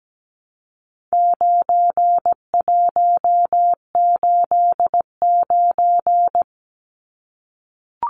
Morse code, heard as a keyed tone, about 17 wpm, 700 Hz. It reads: 9189
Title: Numbers - 4-Digit 17wpm